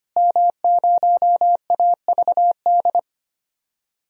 Morse code, heard as a keyed tone, about 25 wpm, 700 Hz. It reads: M0AVD
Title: DXCC-100 Call Signs - 5 Characters - Repeated 2x - 25wpm